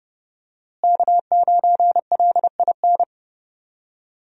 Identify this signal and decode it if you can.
Morse code, heard as a keyed tone, about 32 wpm, 700 Hz. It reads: K9LIN
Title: USA Call Signs - 5 Characters - Repeated 3x - 30wpm